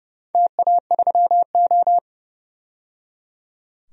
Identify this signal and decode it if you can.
Morse code, heard as a keyed tone, about 30 wpm, 700 Hz. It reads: TA3O
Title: DXCC-100 Call Signs - 4 Characters - 30wpm